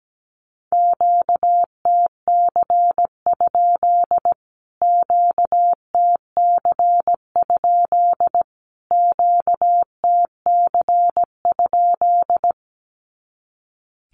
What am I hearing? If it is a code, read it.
Morse code, heard as a keyed tone, about 17 wpm, 700 Hz. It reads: QTC? QTC? QTC?
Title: Q-Codes Repeated 3x 17wpm